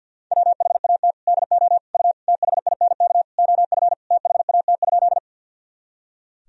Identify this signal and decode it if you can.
Morse code, heard as a keyed tone, about 49 wpm, 700 Hz. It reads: WHAT DO U THINK OF THAT?
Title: Sentences from Top 200 Words and Abbreviations 50wpm